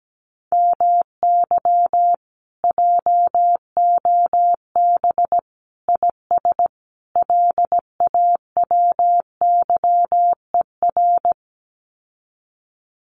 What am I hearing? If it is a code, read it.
Morse code, heard as a keyed tone, about 17 wpm, 700 Hz. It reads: MY JOB IS LAWYER
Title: Common QSO Phrases Encore 17wpm